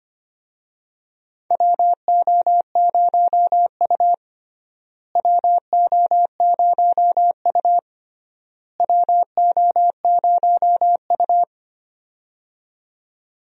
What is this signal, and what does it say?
Morse code, heard as a keyed tone, about 25 wpm, 700 Hz. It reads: WO0U WO0U WO0U
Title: USA Call Signs - 4 Characters - Repeated 3x - 25wpm